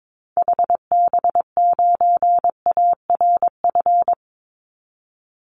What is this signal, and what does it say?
Morse code, heard as a keyed tone, about 22 wpm, 700 Hz. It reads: HB9ARF
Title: Call Signs - 22wpm